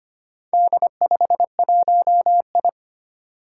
Morse code, heard as a keyed tone, about 25 wpm, 700 Hz. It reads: D51I
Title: Four Challenging Letter-Number Combinations 25wpm